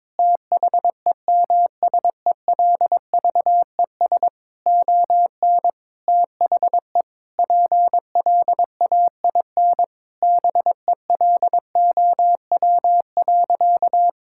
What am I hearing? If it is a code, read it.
Morse code, heard as a keyed tone, about 22 wpm, 700 Hz. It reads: THEMSELVES ON THE PLAIN BELOW.
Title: Sentences from Top 1000 Words - Review Part 5 - 22wpm